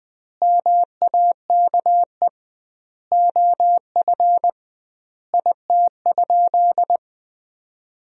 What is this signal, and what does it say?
Morse code, heard as a keyed tone, about 20 wpm, 700 Hz. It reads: MAKE OF IT?
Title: 2x Word-Spacing - Sentences from Top 100 Words - 20wpm